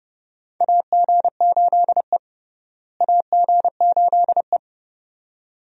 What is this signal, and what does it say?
Morse code, heard as a keyed tone, about 30 wpm, 700 Hz. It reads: AG8E AG8E
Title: USA Call Signs - 4 Characters - Repeated 2x - 30wpm